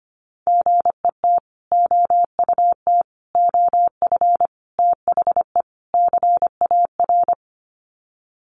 Morse code, heard as a keyed tone, about 25 wpm, 700 Hz. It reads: GET OUT OF THE CAR
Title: Sets of 5 Words Encore2 25wpm